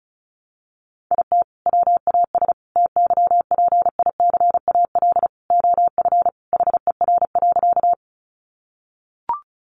Morse code, heard as a keyed tone, about 34 wpm, 700 Hz. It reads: IT WAS TYPICAL OF HER.
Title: Sentences from Top 1100 Words 35wpm